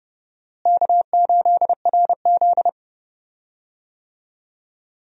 Morse code, heard as a keyed tone, about 30 wpm, 700 Hz. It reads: K8RZ
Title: USA Call Signs - 4 Characters - Repeated 2x - 30wpm